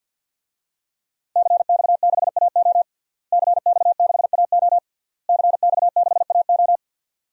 Morse code, heard as a keyed tone, about 51 wpm, 700 Hz. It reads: CX6AO CX6AO CX6AO
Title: DXCC-100 Call Signs - 5 Characters - Repeated 3x - 50wpm